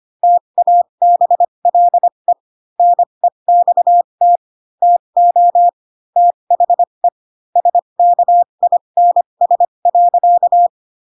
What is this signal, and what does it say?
Morse code, heard as a keyed tone, about 25 wpm, spaced as for 22 wpm, 700 Hz. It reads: TABLE NEXT TO THE SKINS.
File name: Pinocchio / Pinocchio ch7 18 25 ik1hge